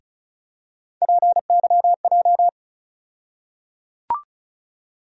Morse code, heard as a keyed tone, about 36 wpm, 700 Hz. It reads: PYJ
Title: USA Call Sign Postfixes - 3 Letters 35wpm